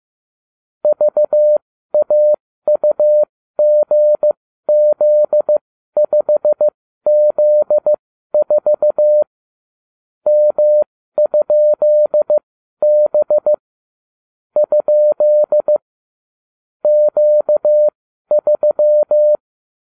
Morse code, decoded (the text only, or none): VAUGZ5Z4 M?B ? Q3